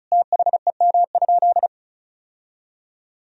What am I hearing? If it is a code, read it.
Morse code, heard as a keyed tone, about 34 wpm, 700 Hz. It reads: THEM?
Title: Sentences from Top 1000 Words - Review Part 1 - 35wpm